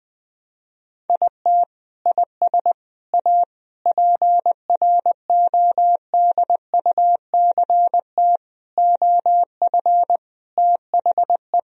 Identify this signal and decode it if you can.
Morse code, heard as a keyed tone, about 20 wpm, 700 Hz. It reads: IT IS A PRODUCT OF THE
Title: Sentences from Top 1000 Words - Review Part 4 - 20wpm